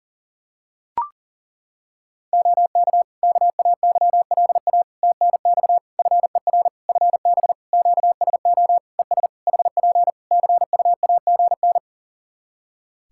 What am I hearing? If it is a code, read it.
Morse code, heard as a keyed tone, about 40 wpm, 700 Hz. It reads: OK KAYLA TNX FER FB QSO ES HP CUAGN
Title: Practice QSO - Exchange 4 - Ending - 40wpm